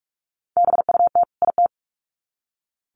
Morse code, heard as a keyed tone, about 44 wpm, 700 Hz. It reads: BUT IT
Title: Sets of 2 Words 45wpm